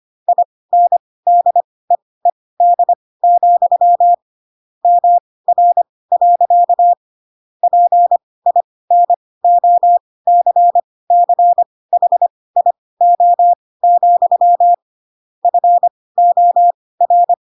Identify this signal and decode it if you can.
Morse code, heard as a keyed tone, about 25 wpm, spaced as for 18 wpm, 700 Hz. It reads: INDEED, MR. PINOCCHIO, FOR
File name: Pinocchio / Pinocchio ch5 12 25 ik1hge